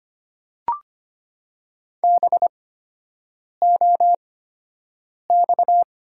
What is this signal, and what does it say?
Morse code, heard as a keyed tone, about 25 wpm, spaced as for 7 wpm, 700 Hz. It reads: BOX
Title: 8x Character Spacing - Top 500 Words - 25wpm